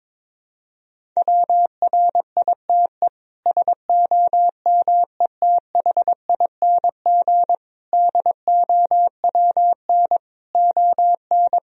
Morse code, heard as a keyed tone, about 22 wpm, 700 Hz. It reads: WRITE SOMETHING DOWN ON
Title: Sentences from Top 1000 Words - Review Part 1 - 22wpm